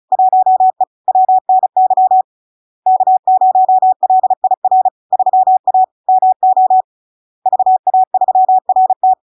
Morse code, heard as a keyed tone, about 36 wpm, 750 Hz. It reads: 1E WNY K0LIR 3A MO VA3RT
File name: LARK-FD-2022-Log-in-CW-35wpm-750-Hz